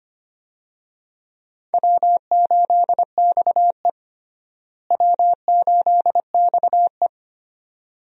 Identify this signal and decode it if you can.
Morse code, heard as a keyed tone, about 25 wpm, 700 Hz. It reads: W8XE W8XE
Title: USA Call Signs - 4 Characters - Repeated 2x - 25wpm